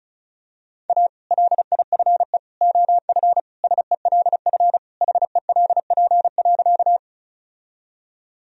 Morse code, heard as a keyed tone, about 34 wpm, 700 Hz. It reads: A LIFE OF SELF HELP.
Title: Sentences from Top 1000 Words - Review Part 1 - 35wpm